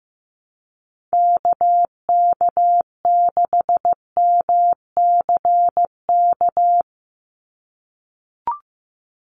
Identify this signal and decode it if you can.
Morse code, heard as a keyed tone, about 15 wpm, 700 Hz. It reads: KK6MCK
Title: USA Call Signs 15wpm